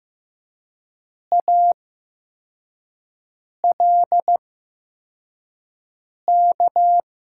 Morse code, heard as a keyed tone, about 15 wpm, spaced as for 4 wpm, 700 Hz. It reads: ALK
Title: 8x Character Spacing - Top 500 Words - 15wpm